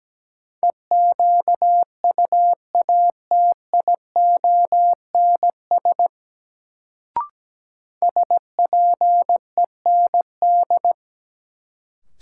Morse code, heard as a keyed tone, about 17 wpm, 700 Hz. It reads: EQUATIONS SPEND
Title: Sets of 5 Words - New Word Drill - 17wpm